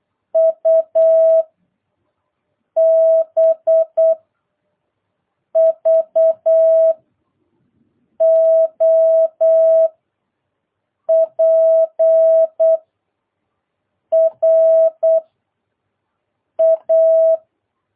Morse code, heard as a keyed tone, about 8 wpm, 650 Hz. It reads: U B V O P R A